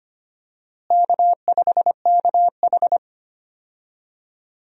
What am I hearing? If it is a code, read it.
Morse code, heard as a keyed tone, about 25 wpm, 700 Hz. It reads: K5KH
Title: USA Call Signs - 4 Characters - Repeated 2x - 25wpm